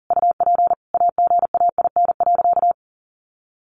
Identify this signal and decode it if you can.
Morse code, heard as a keyed tone, about 40 wpm, 700 Hz. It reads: UP AGAIN.
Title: Sentences from Top 1000 Words 40wpm